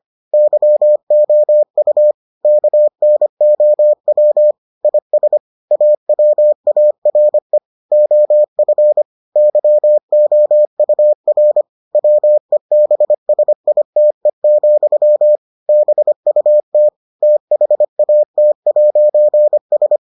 Morse code, heard as a keyed tone, about 25 wpm, 600 Hz. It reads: YOU KNOW IS AWARE OF YOUR WEBSITE, BUT THAT'S